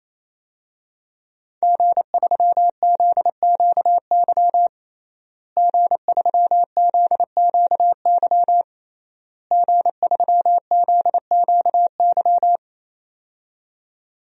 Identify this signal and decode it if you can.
Morse code, heard as a keyed tone, about 28 wpm, 700 Hz. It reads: G3ZQY G3ZQY G3ZQY
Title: DXCC-100 Call Signs - 5 Characters - Repeated 3x - 28wpm